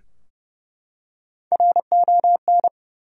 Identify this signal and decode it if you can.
Morse code, heard as a keyed tone, about 30 wpm, 700 Hz. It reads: RON
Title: USA Common Names - Max 4 Letters - 30wpm